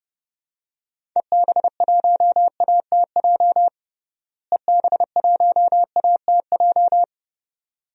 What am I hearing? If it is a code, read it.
Morse code, heard as a keyed tone, about 32 wpm, 700 Hz. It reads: EB1ATJ EB1ATJ
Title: DXCC-100 Call Signs - 6 Characters - Repeated 2x - 30wpm